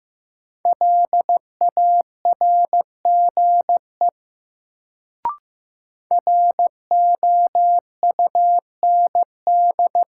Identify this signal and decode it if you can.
Morse code, heard as a keyed tone, about 15 wpm, 700 Hz. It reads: LARGE ROUND
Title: Sentences from Top 200 Words - New Word Drill - 15wpm